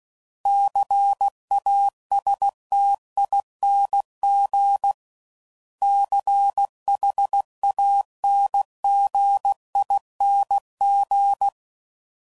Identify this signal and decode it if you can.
Morse code, heard as a keyed tone, about 16 wpm, 800 Hz. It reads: CASTING CHANGING
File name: Suffixes / Special INGs / Sngl Ings